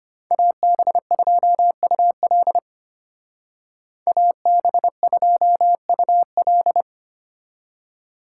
Morse code, heard as a keyed tone, about 33 wpm, 700 Hz. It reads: AB2UL AB2UL